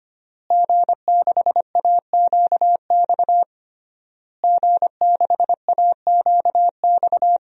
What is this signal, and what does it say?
Morse code, heard as a keyed tone, about 25 wpm, 700 Hz. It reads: G6AQX G6AQX
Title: DXCC-100 Call Signs - 5 Characters - Repeated 3x - 25wpm